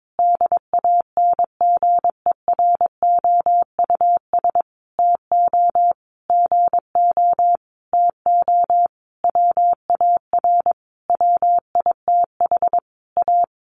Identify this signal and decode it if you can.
Morse code, heard as a keyed tone, about 22 wpm, 700 Hz. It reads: DANGEROUS TO GO TO WAR WITH A
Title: Sentences from Top 1200 Words 22wpm